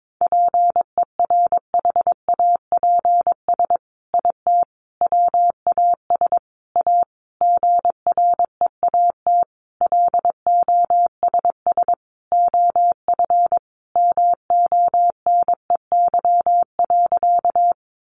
Morse code, heard as a keyed tone, about 22 wpm, 700 Hz. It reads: PERHAPS IT WAS A GREAT LOSS OF MONEY.
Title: Sentences from Top 1400 Words 22wpm